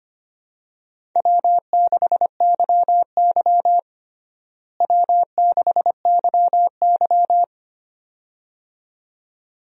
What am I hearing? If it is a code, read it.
Morse code, heard as a keyed tone, about 25 wpm, 700 Hz. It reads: W6YY W6YY
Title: USA Call Signs - 4 Characters - Repeated 2x - 25wpm